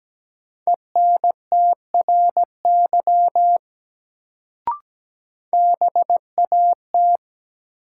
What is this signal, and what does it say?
Morse code, heard as a keyed tone, about 17 wpm, 700 Hz. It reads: ENTRY BAT
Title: Sentences from Top 1500 Words - New Word Drill Alt - 17wpm